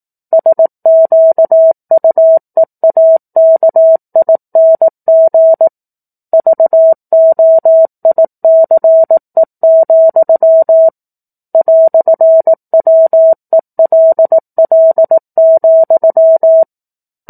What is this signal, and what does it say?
Morse code, heard as a keyed tone, about 18 wpm, 650 Hz. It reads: SQUEAKING VOICE, "WELL,
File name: 04-Chapter-04 18 WPM